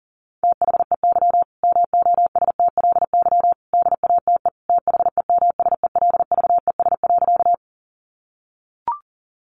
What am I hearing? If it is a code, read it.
Morse code, heard as a keyed tone, about 40 wpm, 700 Hz. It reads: THEY MOSTLY DATE THEMSELVES.
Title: Sentences from Top 1300 Words 40wpm